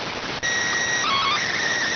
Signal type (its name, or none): SSTV (slow-scan television)